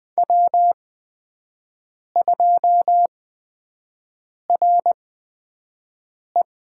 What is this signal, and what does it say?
Morse code, heard as a keyed tone, about 20 wpm, spaced as for 5 wpm, 700 Hz. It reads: W2RE
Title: USA Call Signs - 5 Characters - 8x Character Spacing - 20wpm